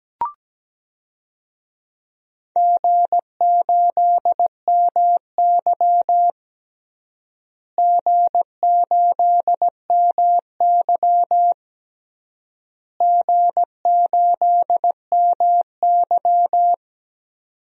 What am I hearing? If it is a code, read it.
Morse code, heard as a keyed tone, about 17 wpm, 700 Hz. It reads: G8MY G8MY G8MY
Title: DXCC-100 Call Signs - 4 Characters - Repeated 3x - 17wpm